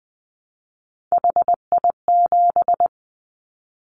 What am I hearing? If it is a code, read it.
Morse code, heard as a keyed tone, about 20 wpm, 700 Hz. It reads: HI7
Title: Callsign Prefixes for Top 100 DXCC - 20wpm